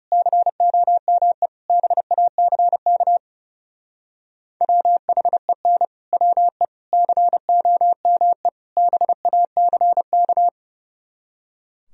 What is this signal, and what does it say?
Morse code, heard as a keyed tone, about 36 wpm, 700 Hz. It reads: COME BACK WHEN WE COME BACK
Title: Speed-Racing - Sets of 4 Words 30wpm